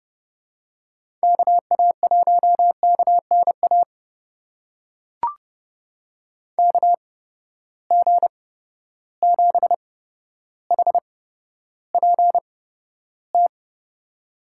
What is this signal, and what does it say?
Morse code, heard as a keyed tone, about 30 wpm, spaced as for 13 wpm, 700 Hz. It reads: KA1KNA KG7HPT
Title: USA Call Signs - 6 Characters - 8x Character Spacing - 30wpm